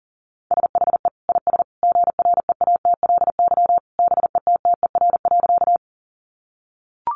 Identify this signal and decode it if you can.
Morse code, heard as a keyed tone, about 40 wpm, 700 Hz. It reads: SHE IS GREATLY BETTER.
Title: Sentences from Top 1400 Words 40wpm